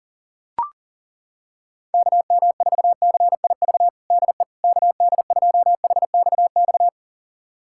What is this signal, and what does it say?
Morse code, heard as a keyed tone, about 37 wpm, 700 Hz. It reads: KM4CIV DE KD2HXX
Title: USA Callsign DE Callsign 40wpm